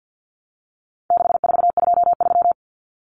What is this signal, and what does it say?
Morse code, heard as a keyed tone, about 50 wpm, 700 Hz. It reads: 6423
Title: Numbers - 4-Digit 50wpm